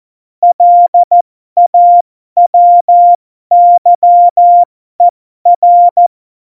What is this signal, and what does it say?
Morse code, heard as a keyed tone, about 14 wpm, spaced as for 12 wpm, 700 Hz. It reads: LAWYER
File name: Morsekurs / Witze